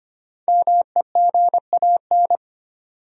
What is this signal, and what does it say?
Morse code, heard as a keyed tone, about 25 wpm, 700 Hz. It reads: MEGAN
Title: USA Common Names - 25wpm